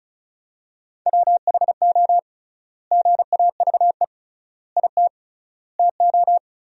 Morse code, heard as a keyed tone, about 34 wpm, 700 Hz. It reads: WHO GAVE IT TO
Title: 3x Word-Spacing - Sentences from Top 500 Words - 35wpm